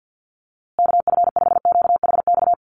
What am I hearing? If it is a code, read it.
Morse code, heard as a keyed tone, about 50 wpm, 700 Hz. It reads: KF5QHX
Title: USA Call Signs Repeated 2x 50wpm